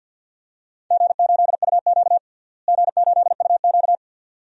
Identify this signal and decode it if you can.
Morse code, heard as a keyed tone, about 51 wpm, 700 Hz. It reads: G8UQ G8UQ
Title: DXCC-100 Call Signs - 4 Characters - Repeated 2x - 50wpm